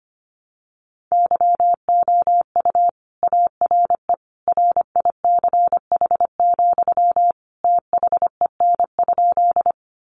Morse code, heard as a keyed tone, about 25 wpm, 700 Hz. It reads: YOU ARE RICH, THEN?
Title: Sentences from Top 1100 Words 25wpm